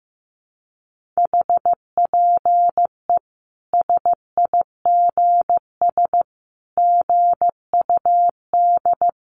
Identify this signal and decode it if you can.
Morse code, heard as a keyed tone, about 15 wpm, 700 Hz. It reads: HPE SIGS GUD
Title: Common QSO Phrases 15wpm